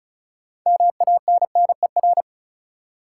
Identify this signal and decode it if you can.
Morse code, heard as a keyed tone, about 36 wpm, 700 Hz. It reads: MANNER
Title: Six Letter Words - 35wpm